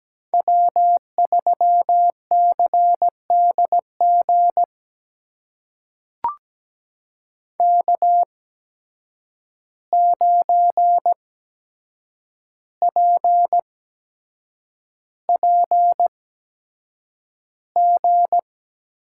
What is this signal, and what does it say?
Morse code, heard as a keyed tone, about 17 wpm, spaced as for 7 wpm, 700 Hz. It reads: W3CDG K9PPG
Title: USA Call Signs - 5 Characters - 8x Character Spacing - 17wpm